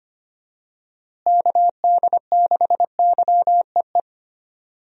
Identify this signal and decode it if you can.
Morse code, heard as a keyed tone, about 25 wpm, 700 Hz. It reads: KD6YEE
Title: USA Call Signs - 6 Characters - Repeated 3x - 25wpm